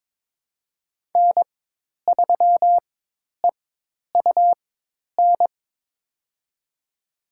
Morse code, heard as a keyed tone, about 22 wpm, spaced as for 10 wpm, 700 Hz. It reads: N3EUN
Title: USA Call Signs - 5 Characters - 4x Character Spacing - 22wpm